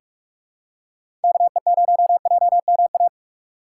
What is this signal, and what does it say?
Morse code, heard as a keyed tone, about 44 wpm, 700 Hz. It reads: KE0JMA
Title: USA Call Signs - 6 Characters - 4x Character Spacing - 45wpm